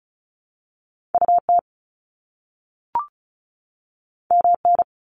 Morse code, heard as a keyed tone, about 35 wpm, 700 Hz. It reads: UT MN
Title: US States 35wpm